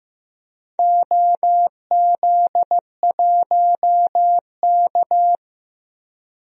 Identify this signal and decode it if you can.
Morse code, heard as a keyed tone, about 15 wpm, 700 Hz. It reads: OZ1K
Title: DXCC-100 Call Signs - 4 Characters - Repeated 2x - 15wpm